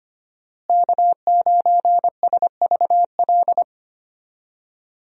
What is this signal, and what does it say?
Morse code, heard as a keyed tone, about 25 wpm, 700 Hz. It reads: K9SVL
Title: USA Call Signs - 5 Characters - Repeated 3x - 25wpm